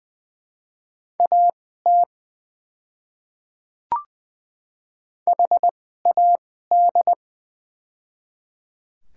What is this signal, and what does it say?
Morse code, heard as a keyed tone, about 20 wpm, spaced as for 14 wpm, 700 Hz. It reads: AT HAD
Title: 2x Character Spacing - Top 200 Words - 20wpm